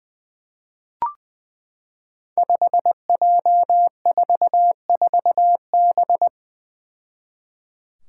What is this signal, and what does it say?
Morse code, heard as a keyed tone, about 20 wpm, 700 Hz. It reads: 5J44B
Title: Five Challenging Letter-Number Combinations 20wpm